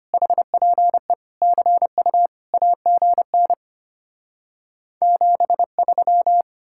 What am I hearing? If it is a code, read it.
Morse code, heard as a keyed tone, about 33 wpm, 700 Hz. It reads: HPE CU AGN 73